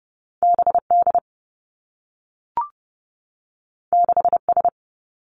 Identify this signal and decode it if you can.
Morse code, heard as a keyed tone, about 30 wpm, 700 Hz. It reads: BD 6S
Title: Two Challenging Letter-Number Combinations 30wpm